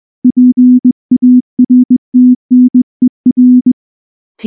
Morse code, heard as a keyed tone, about 20 wpm, 250 Hz. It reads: PARTNER
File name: morse / morse-training-20WPM-2